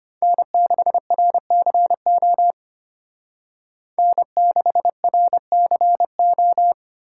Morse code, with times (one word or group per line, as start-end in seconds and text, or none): 0.2-2.5 N6RCO
4.0-6.7 N6RCO